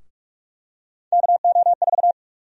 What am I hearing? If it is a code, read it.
Morse code, heard as a keyed tone, about 46 wpm, 700 Hz. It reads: KO4
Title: USA Call Sign Prefixes - Repeated 2x - 45wpm